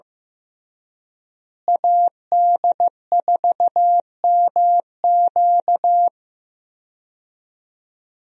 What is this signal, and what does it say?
Morse code, heard as a keyed tone, about 15 wpm, 700 Hz. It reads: AD4MQ
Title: USA Call Signs - 5 Characters - Repeated 3x - 15wpm